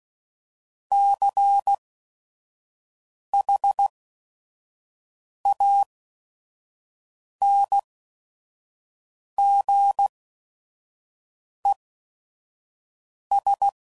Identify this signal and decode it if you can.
Morse code, heard as a keyed tone, about 16 wpm, spaced as for 5 wpm, 800 Hz. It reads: CHANGES
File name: CW Code Groups @5WPM / Technical Subjects / Reactance